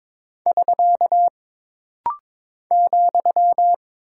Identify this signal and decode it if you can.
Morse code, heard as a keyed tone, about 22 wpm, 700 Hz. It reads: <SK> ,
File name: practice / 22wpm / Single Punctuation-Prosign - Rapid-Fire - 22wpm